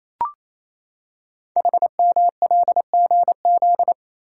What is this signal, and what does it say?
Morse code, heard as a keyed tone, about 27 wpm, 700 Hz. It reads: HMLGZ
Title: Code Groups - 5 Letters - 28wpm